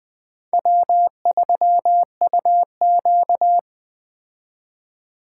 Morse code, heard as a keyed tone, about 20 wpm, 700 Hz. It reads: W3UQ
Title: Speed Racing - USA 1x2 Call Signs 17wpm